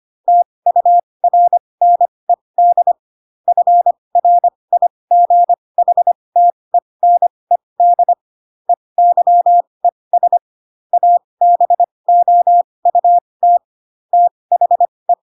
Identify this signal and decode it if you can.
Morse code, heard as a keyed tone, about 25 wpm, spaced as for 20 wpm, 700 Hz. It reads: TURNED FRIGHTENED EYES ABOUT THE
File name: Pinocchio / Pinocchio ch1 15 25 ik1hge